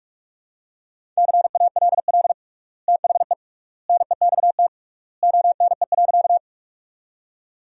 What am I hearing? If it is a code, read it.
Morse code, heard as a keyed tone, about 46 wpm, 700 Hz. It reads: CALL THE NEXT ONE.
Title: 3x Word-Spacing - Sentences from Top 300 Words - 45wpm